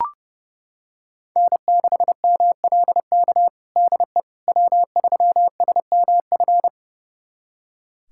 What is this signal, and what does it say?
Morse code, heard as a keyed tone, about 30 wpm, 700 Hz. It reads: N6MLK DE W3SMF
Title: USA Callsign DE Callsign - 5 Characters - 30wpm